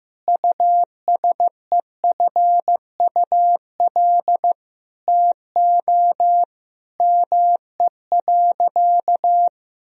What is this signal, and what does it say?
Morse code, heard as a keyed tone, about 15 wpm, 700 Hz. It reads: USEFUL TO ME.
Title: Sentences from Top 1000 Words - Review Part 4 - 15wpm